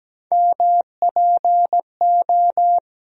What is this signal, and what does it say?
Morse code, heard as a keyed tone, about 17 wpm, 700 Hz. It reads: MPO
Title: USA Call Sign Postfixes - 3 Letters 17wpm